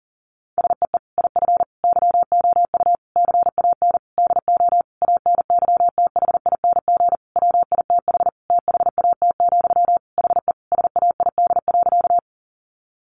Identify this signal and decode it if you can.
Morse code, heard as a keyed tone, about 40 wpm, 700 Hz. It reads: SEE IF YOU CAN DO ANYTHING WITH THAT, HE SAID.
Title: Sentences from Top 1000 Words 40wpm